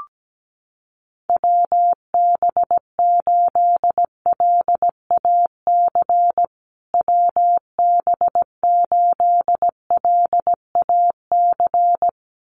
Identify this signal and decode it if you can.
Morse code, heard as a keyed tone, about 17 wpm, 700 Hz. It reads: WB8LAC WB8LAC
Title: USA Call Signs Repeated 2x 17wpm